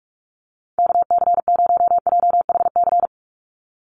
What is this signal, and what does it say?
Morse code, heard as a keyed tone, about 44 wpm, 690 Hz. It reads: KC0JHC
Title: USA Call Signs 45wpm